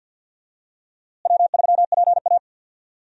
Morse code, heard as a keyed tone, about 48 wpm, 700 Hz. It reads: W3PA
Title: USA Call Signs - 4 Characters - Repeated 3x - 50wpm